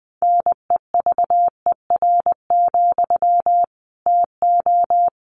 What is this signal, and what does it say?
Morse code, heard as a keyed tone, about 20 wpm, 700 Hz. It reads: NEVER, TO
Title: Sentences from Top 900 Words 20wpm